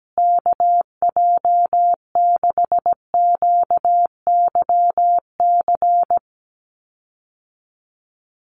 Morse code, heard as a keyed tone, about 17 wpm, 700 Hz. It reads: KJ6QYC
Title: USA Call Signs Repeated 2x 17wpm